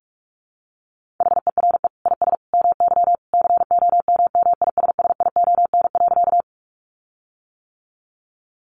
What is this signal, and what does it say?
Morse code, heard as a keyed tone, about 44 wpm, 700 Hz. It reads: HERE IS MY COMMISSION.
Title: Sentences from Top 1400 Words 45wpm